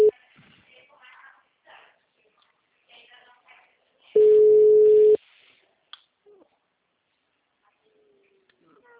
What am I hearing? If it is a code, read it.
Ringback tone (Europe, 425 Hz), with speech.
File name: out- -2011- - - .37786